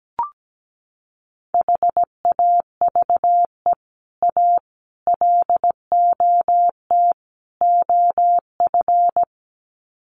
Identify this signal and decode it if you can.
Morse code, heard as a keyed tone, about 17 wpm, 700 Hz. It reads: HAVE A LOT OF
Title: Sets of 4 Words 17wpm